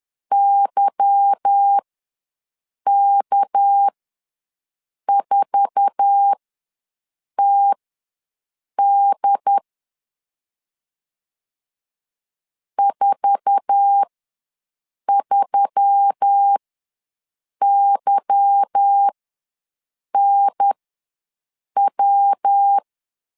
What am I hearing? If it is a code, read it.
Morse code, heard as a keyed tone, about 10 wpm, spaced as for 6 wpm, 800 Hz. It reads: YK4TD 43YNW
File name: LEZIONI / 8 YJ45 / 8 YJ4 ws 10wpm 4020